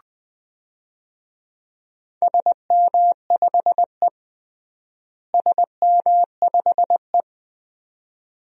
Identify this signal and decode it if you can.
Morse code, heard as a keyed tone, about 20 wpm, 700 Hz. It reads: SM5E SM5E
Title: DXCC-100 Call Signs - 4 Characters - Repeated 2x - 20wpm